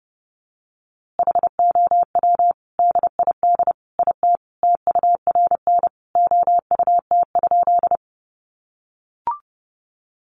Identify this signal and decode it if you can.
Morse code, heard as a keyed tone, about 30 wpm, 700 Hz. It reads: HOW DID IT TURN OUT?
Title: A Sound Of Thunder 30wpm